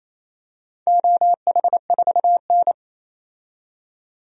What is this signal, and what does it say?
Morse code, heard as a keyed tone, about 28 wpm, 700 Hz. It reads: OH4N
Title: DXCC-100 Call Signs - 4 Characters - 28wpm